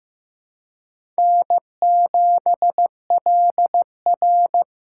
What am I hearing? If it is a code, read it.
Morse code, heard as a keyed tone, about 15 wpm, 700 Hz. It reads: N7LR
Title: K1USN SST QSO Practice - 15wpm